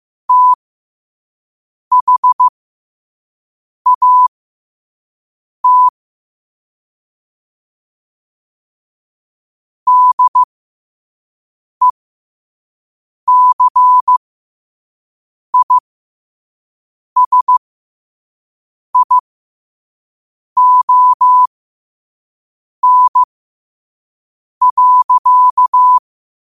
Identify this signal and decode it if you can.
Morse code, heard as a keyed tone, about 15 wpm, spaced as for 5 wpm, 1000 Hz. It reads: THAT DECISION.
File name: qotdfiles / 5wpm